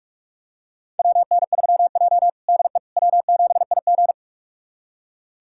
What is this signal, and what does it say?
Morse code, heard as a keyed tone, about 41 wpm, 700 Hz. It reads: WN3J DE W7IG
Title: USA Callsign DE Callsign - 4 Characters - 45wpm